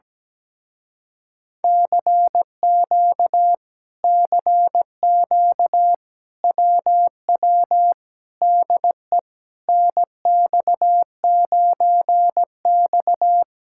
Morse code, heard as a keyed tone, about 17 wpm, 700 Hz. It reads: CQ CQ WW DE NX9X
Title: Speed-Racing Common QSO Phrases - 17wpm